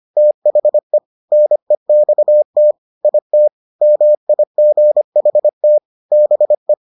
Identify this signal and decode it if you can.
Morse code, heard as a keyed tone, about 25 wpm, 600 Hz. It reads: THE NEXT IT MIGHT BE